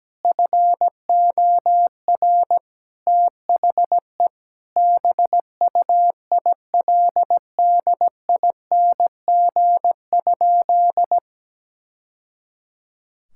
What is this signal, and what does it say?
Morse code, heard as a keyed tone, about 17 wpm, 700 Hz. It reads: FOR THE BUILDING?
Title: Sentences from Top 1000 Words - Review Part 5 - 17wpm